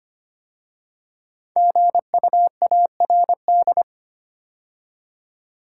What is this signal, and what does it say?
Morse code, heard as a keyed tone, about 25 wpm, 700 Hz. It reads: GUARD
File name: 5 Letter Words - First 3 Letters Given - 25wpm